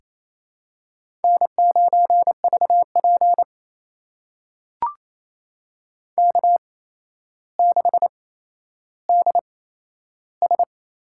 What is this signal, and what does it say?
Morse code, heard as a keyed tone, about 28 wpm, spaced as for 12 wpm, 700 Hz. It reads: N9VP K6DS
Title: USA Call Signs - 4 Characters - 8x Character Spacing - 28wpm